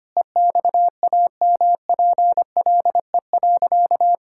Morse code, heard as a keyed tone, about 25 wpm, 700 Hz. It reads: EXAMPLE.
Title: Sentences from Top 1000 Words - Review Part 5 - 25wpm